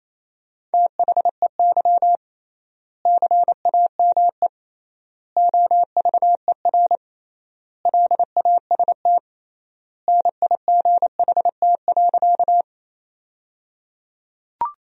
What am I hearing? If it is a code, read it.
Morse code, heard as a keyed tone, about 28 wpm, 700 Hz. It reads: THEY CAME OVER LAST NIGHT.
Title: 3x Word-Spacing - Sentences from Top 300 Words - 28wpm